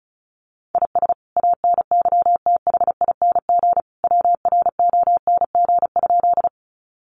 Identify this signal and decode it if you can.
Morse code, heard as a keyed tone, about 34 wpm, 700 Hz. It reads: IS ANYTHING WRONG?
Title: Sentences from Top 1000 Words 35wpm